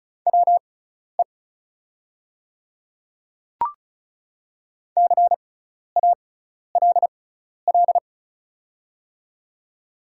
Morse code, heard as a keyed tone, about 36 wpm, spaced as for 12 wpm, 700 Hz. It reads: WE CALL
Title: 6x Character Spacing - Top 200 Words - 35wpm